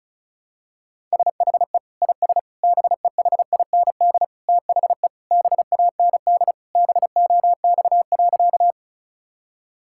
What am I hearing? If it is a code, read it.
Morse code, heard as a keyed tone, about 36 wpm, 700 Hz. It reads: SHE IS BEHIND THE BAND BOX.
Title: Sentences from Top 1600 Words 35wpm